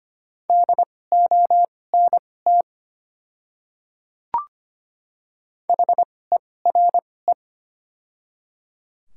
Morse code, heard as a keyed tone, about 25 wpm, spaced as for 18 wpm, 700 Hz. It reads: DONT HERE
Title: 2x Character Spacing - Top 500 Words - 25wpm